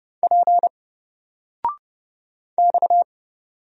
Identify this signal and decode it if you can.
Morse code, heard as a keyed tone, about 31 wpm, 700 Hz. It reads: P X
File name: Single-Letter-Number-30wpm